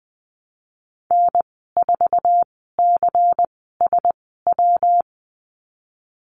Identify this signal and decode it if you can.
Morse code, heard as a keyed tone, about 20 wpm, spaced as for 15 wpm, 700 Hz. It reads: N4CSW
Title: 161 Lesson 26g - Letter g - Call Signs 20wpm